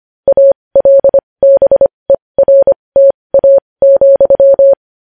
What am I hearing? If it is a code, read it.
Morse code, heard as a keyed tone, about 25 wpm, spaced as for 20 wpm, 550 Hz. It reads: ALBERTA,
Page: news headlines in morse code at 20 wpm